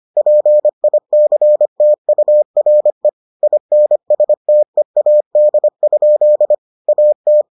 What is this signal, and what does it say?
Morse code, heard as a keyed tone, about 25 wpm, 600 Hz. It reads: PICTURE INSTEAD? AT